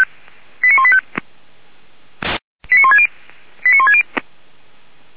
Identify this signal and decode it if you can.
Five-tone selective call (ZVEI), heard as quick runs of five tones, tones of about 80 ms. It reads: 89185 89160 89160